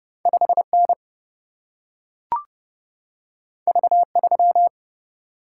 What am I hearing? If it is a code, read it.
Morse code, heard as a keyed tone, about 30 wpm, 700 Hz. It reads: 5 N V 3
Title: Single-Letter Single-Number 30wpm